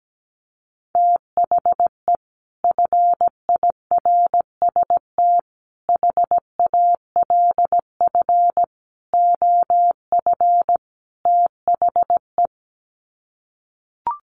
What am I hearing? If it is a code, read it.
Morse code, heard as a keyed tone, about 17 wpm, 700 Hz. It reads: THE FIRST HALF OF THE
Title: Sets of 5 Words 17wpm